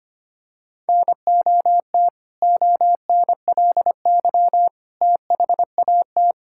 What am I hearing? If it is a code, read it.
Morse code, heard as a keyed tone, about 25 wpm, 700 Hz. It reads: NOT ONLY THAT
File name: practice / 25wpm / Sentences with 2-Word Phrases - 25wpm